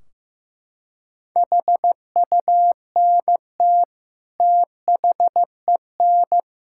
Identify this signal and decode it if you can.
Morse code, heard as a keyed tone, about 15 wpm, 700 Hz. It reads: HUNT THEN
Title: Sentences from Top 1600 Words 15wpm